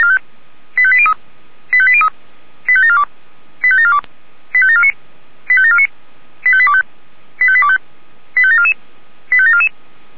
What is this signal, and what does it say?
Five-tone selective call (ZVEI), heard as quick runs of five tones, tones of about 70 ms. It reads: 85803 85803 85742 85742 85749 85749 85825 85825 85744 85744